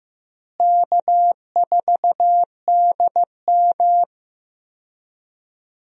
Morse code, heard as a keyed tone, about 16 wpm, 700 Hz. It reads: K4DM
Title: USA Call Signs - 4 Characters - Repeated 2x - 15wpm